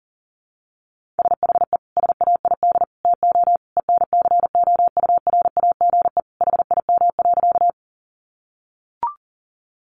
Morse code, heard as a keyed tone, about 40 wpm, 700 Hz. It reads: SHE SAID TO ENCOURAGE HIM.
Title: Sentences from Top 1400 Words 40wpm